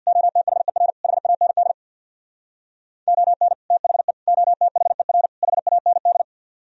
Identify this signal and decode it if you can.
Morse code, heard as a keyed tone, about 64 wpm, 700 Hz. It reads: OTHER HAND ON THE OTHER HAND